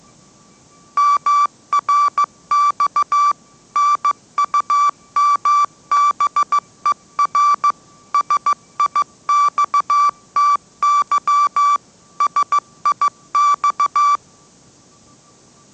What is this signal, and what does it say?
Morse code, heard as a keyed tone, about 15 wpm, 1160 Hz. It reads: MRX NUMBER SIXTY SIX
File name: mrx / mrx066 16wpm